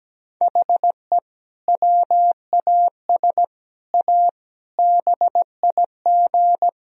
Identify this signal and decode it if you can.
Morse code, heard as a keyed tone, about 17 wpm, 700 Hz. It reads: HE WAS A BIG
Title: Sentences from Top 1000 Words - Review Part 1 - 17wpm